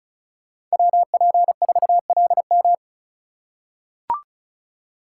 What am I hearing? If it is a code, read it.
Morse code, heard as a keyed tone, about 34 wpm, 700 Hz. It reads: WP4LM
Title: USA Call Signs - 5 Characters - 2x Character Spacing - 35wpm